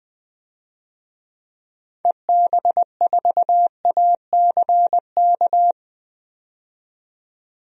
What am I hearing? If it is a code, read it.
Morse code, heard as a keyed tone, about 20 wpm, 700 Hz. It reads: EB4ACK
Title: DXCC-100 Call Signs - 6 Characters - 20wpm